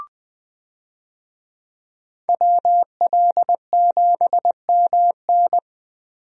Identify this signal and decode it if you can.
Morse code, heard as a keyed tone, about 21 wpm, 700 Hz. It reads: WL7MN
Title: USA Call Signs - 5 Characters - Repeated 3x - 20wpm